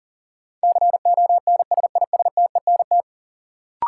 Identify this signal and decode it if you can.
Morse code, heard as a keyed tone, about 40 wpm, 700 Hz. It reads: CONSISTENT
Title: Sentences from Top 1500 Words - New Word Drill Alt - 40wpm